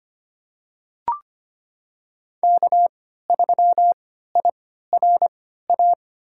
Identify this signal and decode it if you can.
Morse code, heard as a keyed tone, about 25 wpm, spaced as for 14 wpm, 700 Hz. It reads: K3IRA
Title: USA Call Signs - 5 Characters - 3x Character Spacing - 25wpm